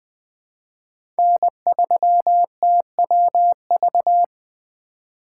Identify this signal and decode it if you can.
Morse code, heard as a keyed tone, about 20 wpm, 700 Hz. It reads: N3TWV
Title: USA Call Signs - 5 Characters - 4x Character Spacing - 20wpm